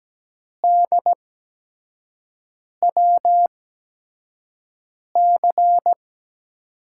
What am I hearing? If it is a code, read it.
Morse code, heard as a keyed tone, about 17 wpm, spaced as for 5 wpm, 700 Hz. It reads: DWC
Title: USA Call Signs - 8x Character Spacing - 17wpm